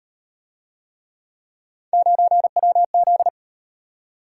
Morse code, heard as a keyed tone, about 37 wpm, 700 Hz. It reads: 9WZ